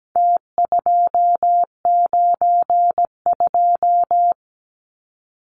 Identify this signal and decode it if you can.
Morse code, heard as a keyed tone, about 17 wpm, 700 Hz. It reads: T292
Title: Numbers - Contest Style - 4-Digit 17wpm